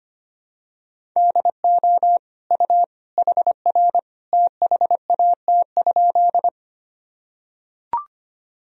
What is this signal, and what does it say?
Morse code, heard as a keyed tone, about 25 wpm, 700 Hz. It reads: DO U HR THAT?
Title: Sentences from Top 300 Words and Abbreviations 25wpm